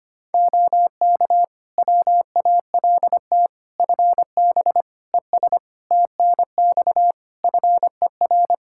Morse code, heard as a keyed tone, about 25 wpm, 700 Hz. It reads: OK WALT FB ES TNX FER
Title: Practice QSO - Exchange 2 - Info - 25wpm